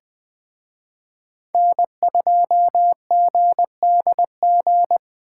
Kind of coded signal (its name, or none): Morse code